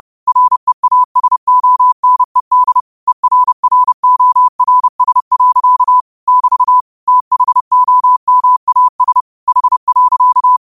Morse code, heard as a keyed tone, about 30 wpm, 1000 Hz. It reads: REASONED ERRORS. <BT> THOMAS H.